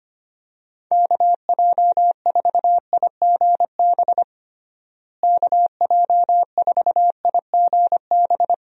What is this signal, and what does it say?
Morse code, heard as a keyed tone, about 25 wpm, 700 Hz. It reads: KJ4IGB KJ4IGB
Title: USA Call Signs - 6 Characters - Repeated 3x - 25wpm